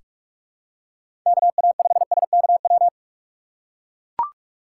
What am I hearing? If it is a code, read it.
Morse code, heard as a keyed tone, about 44 wpm, 700 Hz. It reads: KA5SKW
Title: USA Call Signs - 6 Characters - 6x Character Spacing - 45wpm